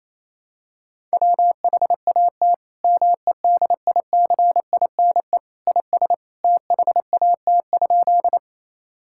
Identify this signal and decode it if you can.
Morse code, heard as a keyed tone, about 28 wpm, 700 Hz. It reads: WHAT MEDICINE IS THAT?
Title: Sentences from Top 1000 Words - Review Part 6 - 28wpm